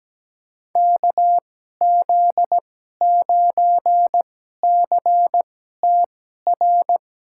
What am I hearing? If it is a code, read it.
Morse code, heard as a keyed tone, about 17 wpm, spaced as for 12 wpm, 700 Hz. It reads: KZ9CTR
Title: USA Call Signs - 6 Characters - 2x Character Spacing - 17wpm